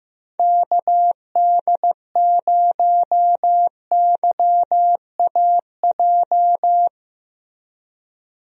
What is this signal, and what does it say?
Morse code, heard as a keyed tone, about 15 wpm, 700 Hz. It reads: KD0YAJ
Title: USA Call Signs - 6 Characters - Repeated 3x - 15wpm